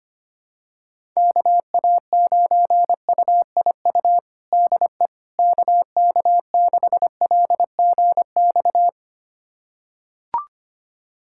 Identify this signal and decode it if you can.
Morse code, heard as a keyed tone, about 25 wpm, 700 Hz. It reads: KA9UIU DE KK6LGX
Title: USA Callsign DE Callsign - 6 Characters - 25wpm